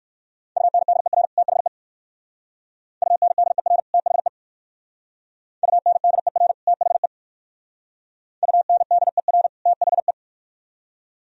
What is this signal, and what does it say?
Morse code, heard as a keyed tone, about 58 wpm, 700 Hz. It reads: UNDER THE UNDER THE UNDER THE UNDER THE